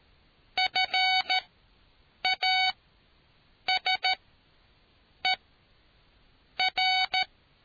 Morse code, heard as a keyed tone, about 13 wpm, spaced as for 6 wpm, 750 Hz. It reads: FASER